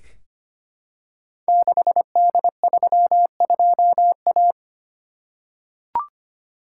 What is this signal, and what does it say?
Morse code, heard as a keyed tone, about 25 wpm, 700 Hz. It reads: 6D32A
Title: Code Groups - 5 Letter:Numbers - 25wpm